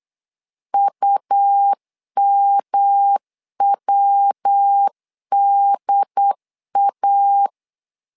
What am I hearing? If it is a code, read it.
Morse code, heard as a keyed tone, about 8 wpm, 800 Hz. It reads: UMWDA
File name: LEZIONI / 3 MWRO / 3 MW 8wpm 2501